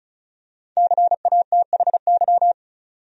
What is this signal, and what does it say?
Morse code, heard as a keyed tone, about 35 wpm, 700 Hz. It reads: CATHY
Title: USA Common Names - Max 5 Letters - 35wpm